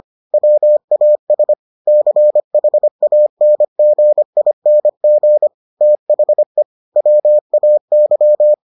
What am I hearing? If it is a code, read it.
Morse code, heard as a keyed tone, about 25 wpm, 600 Hz. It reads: WAS CHANGING THE WAY